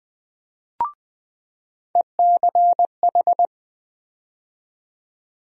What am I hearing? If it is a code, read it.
Morse code, heard as a keyed tone, about 20 wpm, 700 Hz. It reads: ECH
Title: USA Call Sign Postfixes - 3 Letters 20wpm